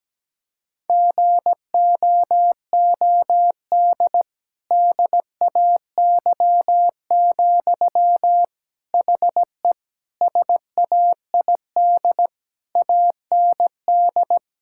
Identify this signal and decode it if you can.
Morse code, heard as a keyed tone, about 17 wpm, 700 Hz. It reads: GOOD DAY, HE SAID AND
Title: Sentences from Top 1600 Words 17wpm